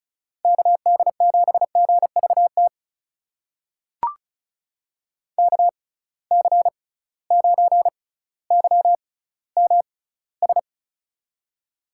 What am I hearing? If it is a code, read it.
Morse code, heard as a keyed tone, about 36 wpm, spaced as for 18 wpm, 700 Hz. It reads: KD7GVT KC9YMS
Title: USA Call Signs - 6x Character Spacing - 35wpm